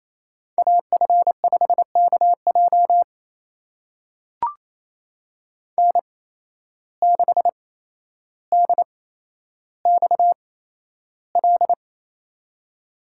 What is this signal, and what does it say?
Morse code, heard as a keyed tone, about 27 wpm, spaced as for 12 wpm, 700 Hz. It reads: AF5KJ N6DXL
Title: USA Call Signs - 5 Characters - 8x Character Spacing - 28wpm